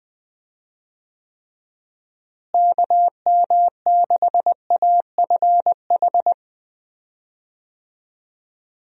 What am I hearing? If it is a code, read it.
Morse code, heard as a keyed tone, about 20 wpm, 700 Hz. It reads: KM6AFH
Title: USA Call Signs - 6 Characters - Repeated 2x - 20wpm